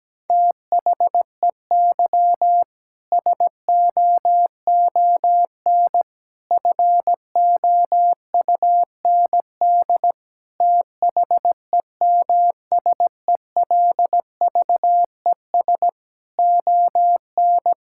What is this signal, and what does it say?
Morse code, heard as a keyed tone, about 17 wpm, 700 Hz. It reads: THEY SOON FOUND THEMSELVES ON
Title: Sentences from Top 1000 Words - Review Part 5 - 17wpm